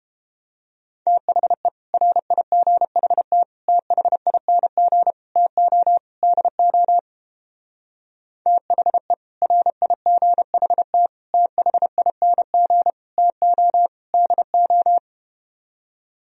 Morse code, heard as a keyed tone, about 31 wpm, 700 Hz. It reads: THE RIGHT THING TO DO THE RIGHT THING TO DO